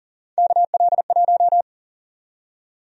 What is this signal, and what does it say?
Morse code, heard as a keyed tone, about 40 wpm, 700 Hz. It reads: KL1
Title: USA Call Sign Prefixes - 40wpm